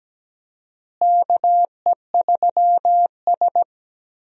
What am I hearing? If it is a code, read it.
Morse code, heard as a keyed tone, about 17 wpm, 700 Hz. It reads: KE3S
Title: USA Call Signs - 4 Characters - Repeated 2x - 17wpm